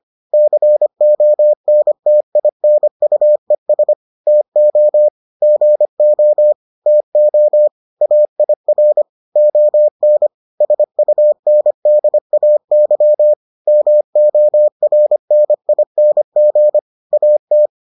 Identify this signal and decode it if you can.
Morse code, heard as a keyed tone, about 25 wpm, 600 Hz. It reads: CONTINUES TO GO TO AIR ON SUNDAY MORNING AT